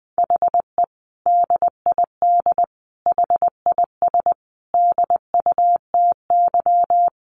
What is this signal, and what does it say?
Morse code, heard as a keyed tone, about 20 wpm, 700 Hz. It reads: HE DID HIS DUTY
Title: Sentences from Top 1500 Words 20wpm